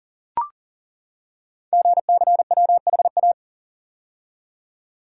Morse code, heard as a keyed tone, about 39 wpm, 700 Hz. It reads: GCWHA
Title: Code Groups - 5 Letter:Numbers - 40wpm